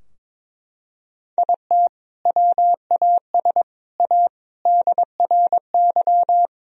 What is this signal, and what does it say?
Morse code, heard as a keyed tone, about 22 wpm, 700 Hz. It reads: IT WAS A DRY
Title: Sentences from Top 1000 Words - Review Part 3 - 22wpm